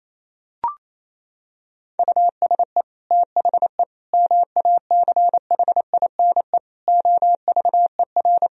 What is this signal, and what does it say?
Morse code, heard as a keyed tone, about 28 wpm, 700 Hz. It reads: USE THE MACHINE OVER
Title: Sentences from Top 1000 Words - Review Part 2 - 28wpm